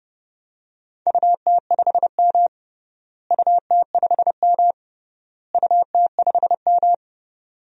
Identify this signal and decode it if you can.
Morse code, heard as a keyed tone, about 30 wpm, 700 Hz. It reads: UT5M UT5M UT5M
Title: DXCC-100 Call Signs - 4 Characters - Repeated 3x - 30wpm